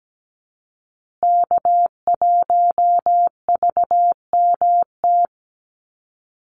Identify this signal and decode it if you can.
Morse code, heard as a keyed tone, about 17 wpm, 700 Hz. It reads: K1VMT
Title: Call Signs Repeated 2x - 17wpm